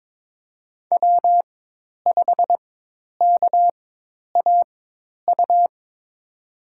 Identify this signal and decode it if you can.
Morse code, heard as a keyed tone, about 22 wpm, spaced as for 10 wpm, 700 Hz. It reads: W5KAU
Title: USA Call Signs - 5 Characters - 4x Character Spacing - 22wpm